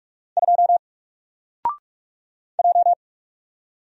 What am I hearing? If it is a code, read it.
Morse code, heard as a keyed tone, about 44 wpm, 700 Hz. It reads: 2 J
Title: Single Letter-Number Alt 45wpm